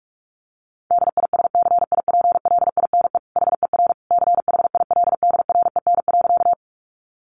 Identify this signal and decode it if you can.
Morse code, heard as a keyed tone, about 44 wpm, 700 Hz. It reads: DISCIPLINE HER CHILDREN.
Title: Sentences from Top 900 Words 45wpm